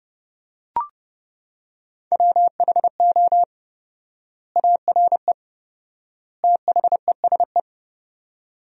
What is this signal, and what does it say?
Morse code, heard as a keyed tone, about 30 wpm, 700 Hz. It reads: WHO ARE THESE
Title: 4x Word-Spacing - Sentences from Top 200 Words - 30wpm